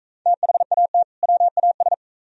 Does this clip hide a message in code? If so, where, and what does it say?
Yes: Morse code from 0.3 to 1.9 s, heard as a keyed tone, about 44 wpm, 700 Hz. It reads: THAT WAS